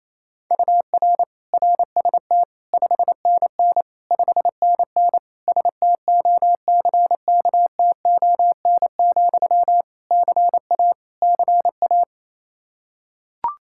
Morse code, heard as a keyed tone, about 27 wpm, 700 Hz. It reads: UR RST 5NN 5NN STOCKTON, CA CA
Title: Practice QSO - Exchange 1 - Beginning - 28wpm